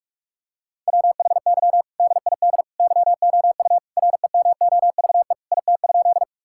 Morse code, heard as a keyed tone, about 46 wpm, 700 Hz. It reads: WHY DID YOU REMOVE IT?
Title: Sentences from Top 1000 Words - Review Part 5 - 45wpm